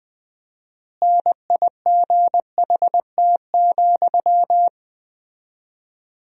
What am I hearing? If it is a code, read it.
Morse code, heard as a keyed tone, about 20 wpm, 700 Hz. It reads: NIGHT,
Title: 4x Word-Spacing - Sentences from Top 300 Words - 20wpm